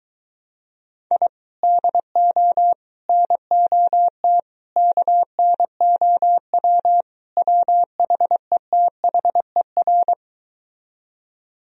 Morse code, heard as a keyed tone, about 23 wpm, 700 Hz. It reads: I DO NOT KNOW WHETHER